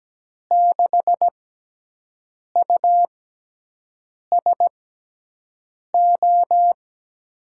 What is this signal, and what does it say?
Morse code, heard as a keyed tone, about 17 wpm, spaced as for 6 wpm, 700 Hz. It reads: 6USO
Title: USA Call Signs - 5 Characters - 6x Character Spacing - 17wpm